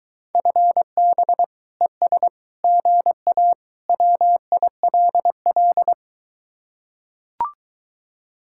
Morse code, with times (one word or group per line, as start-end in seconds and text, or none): 0.3-1.4 FB
1.8-2.3 ES
2.6-3.5 GA
3.9-5.9 WILL